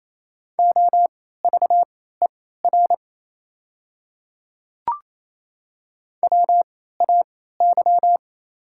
Morse code, heard as a keyed tone, about 27 wpm, spaced as for 16 wpm, 690 Hz. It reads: OVER WAY
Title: 3x Character Spacing - Top 100 Words - 28wpm